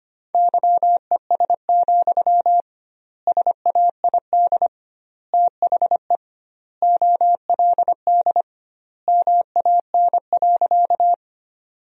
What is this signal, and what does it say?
Morse code, heard as a keyed tone, about 25 wpm, 700 Hz. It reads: YES, SAID THE OLD MAN.
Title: 2x Word-Spacing - Sentences from Top 500 Words - 25wpm